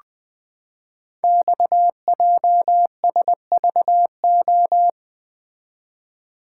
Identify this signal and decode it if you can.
Morse code, heard as a keyed tone, about 20 wpm, 700 Hz. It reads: XJSVO
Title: Code Groups - 5 Letters - 20wpm